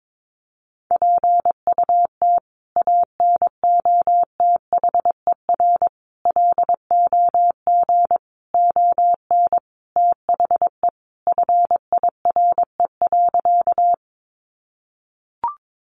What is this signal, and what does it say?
Morse code, heard as a keyed tone, about 22 wpm, 700 Hz. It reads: PUT ANOTHER LOG ON THE FIRE.
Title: Sentences from Top 1400 Words 22wpm